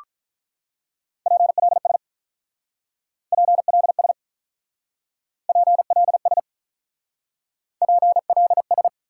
Morse code, heard as a keyed tone, about 44 wpm, 700 Hz. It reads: PLS PLS PLS PLS